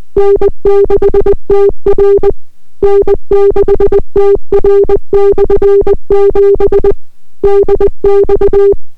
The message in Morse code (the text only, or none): N6TR N6TR/7 DX